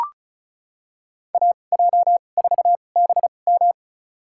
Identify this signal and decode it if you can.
Morse code, heard as a keyed tone, about 34 wpm, spaced as for 25 wpm, 690 Hz. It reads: AJ4BM
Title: USA Call Signs - 5 Characters - 2x Character Spacing - 35wpm